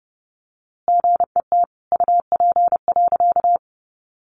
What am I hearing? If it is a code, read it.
Morse code, heard as a keyed tone, about 30 wpm, 700 Hz. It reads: GET UP.
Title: A Sound Of Thunder 30wpm